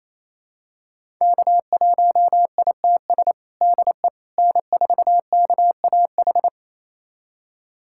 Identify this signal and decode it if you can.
Morse code, heard as a keyed tone, about 28 wpm, 700 Hz. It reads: K1ITS DE N4KAH
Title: USA Callsign DE Callsign - 5 Characters - 28wpm